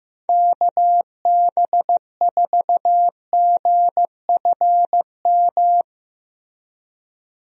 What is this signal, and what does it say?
Morse code, heard as a keyed tone, about 15 wpm, 700 Hz. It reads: KB4GFM
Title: USA Call Signs - 6 Characters - Repeated 2x - 15wpm